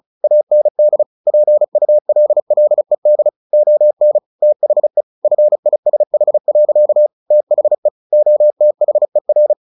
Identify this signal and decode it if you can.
Morse code, heard as a keyed tone, about 35 wpm, 600 Hz. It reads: AND PULLED ON THE FISH. THE OTHER
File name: the old man and the sea 2 35wpm